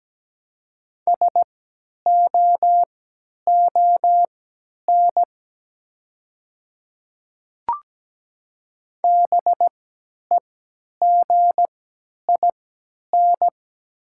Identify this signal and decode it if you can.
Morse code, heard as a keyed tone, about 17 wpm, spaced as for 10 wpm, 700 Hz. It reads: SOON BEGIN
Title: 3x Character Spacing - Top 500 Words - 17wpm